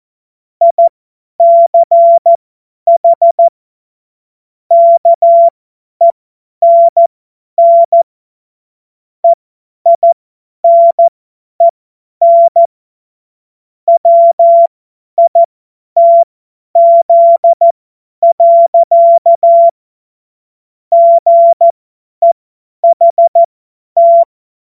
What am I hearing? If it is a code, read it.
Morse code, heard as a keyed tone, about 14 wpm, spaced as for 10 wpm, 680 Hz. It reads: ICH KENN EINEN WITZ. GEHT
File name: Morsekurs / Witze / Witz 20. Juli 2021, 50 F70